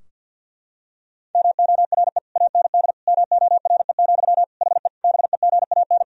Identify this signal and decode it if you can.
Morse code, heard as a keyed tone, about 50 wpm, 700 Hz. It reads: MORE AND MORE, HE BEGAN
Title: Sentences from Top 1000 Words - Review Part 3 - 50wpm